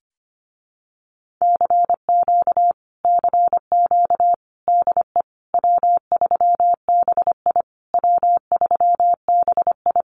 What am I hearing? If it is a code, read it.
Morse code, heard as a keyed tone, about 25 wpm, 700 Hz. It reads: CQ CQ DE W3BI W3BI
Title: Common QSO Phrases Encore 25wpm